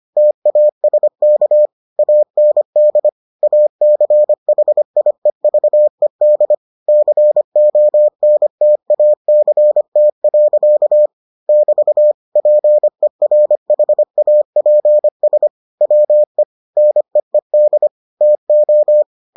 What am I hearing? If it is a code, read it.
Morse code, heard as a keyed tone, about 25 wpm, 600 Hz. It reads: TASK AND ACHIEVED CONTACT. <BT> PERHAPS WE NEED TO